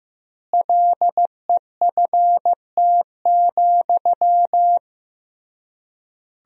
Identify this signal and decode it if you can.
Morse code, heard as a keyed tone, about 16 wpm, 700 Hz. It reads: LEFT,
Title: 3x Word-Spacing - Sentences from Top 400 Words - 15wpm